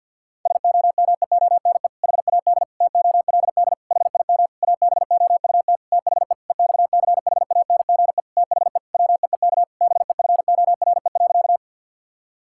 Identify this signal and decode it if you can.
Morse code, heard as a keyed tone, about 50 wpm, 700 Hz. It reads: SOMEONE HAD TOLD HIM ABOUT THE EXCHANGE THE WEEK BEFORE.
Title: Sentences from Top 1000 Words - Review Part 6 - 50wpm